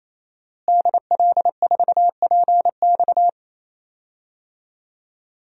Morse code, heard as a keyed tone, about 28 wpm, 700 Hz. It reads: DL4PX
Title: DXCC-100 Call Signs - 5 Characters - Repeated 3x - 28wpm